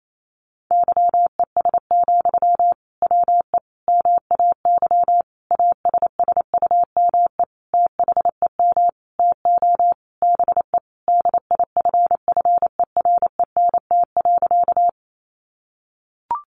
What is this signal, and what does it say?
Morse code, heard as a keyed tone, about 28 wpm, 700 Hz. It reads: YES, WE MAY ASSUME THEM TO BE DIFFERENT.
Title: Sentences from Top 1500 Words 28wpm